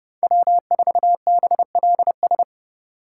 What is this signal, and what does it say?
Morse code, heard as a keyed tone, about 27 wpm, 700 Hz. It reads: W4BLS